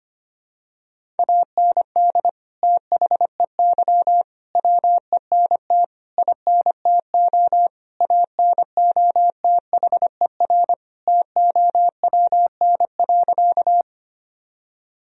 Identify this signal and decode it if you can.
Morse code, heard as a keyed tone, about 25 wpm, 700 Hz. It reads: AND THEY WENT INTO ANOTHER TOWN.
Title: Sentences from Top 1000 Words - Review Part 2 - 25wpm